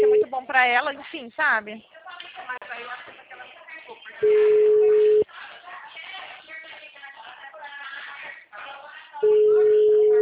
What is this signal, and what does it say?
Ringback tone (Europe, 425 Hz), with speech.